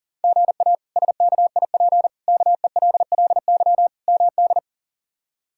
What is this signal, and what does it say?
Morse code, heard as a keyed tone, about 40 wpm, 700 Hz. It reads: GA SKIP KELLY MD
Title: K1USN SST QSO Practice - 40wpm